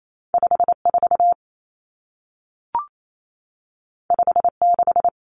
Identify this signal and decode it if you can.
Morse code, heard as a keyed tone, about 28 wpm, 700 Hz. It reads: 54 56
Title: Common Cut Numbers - 2-Digit 28wpm